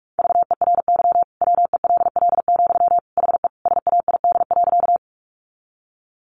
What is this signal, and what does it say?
Morse code, heard as a keyed tone, about 44 wpm, 700 Hz. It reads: VERY WELL, HE SAID.
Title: Sentences from Top 200 Words 45wpm